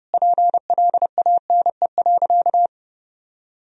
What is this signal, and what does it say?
Morse code, heard as a keyed tone, about 30 wpm, 700 Hz. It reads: PLANE.
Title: Sentences from Top 1000 Words - Review Part 3 - 30wpm